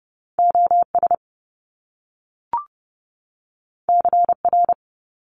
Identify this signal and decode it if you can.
Morse code, heard as a keyed tone, about 30 wpm, 700 Hz. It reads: OS CR
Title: Top Two Letter Combinations in Words 30wpm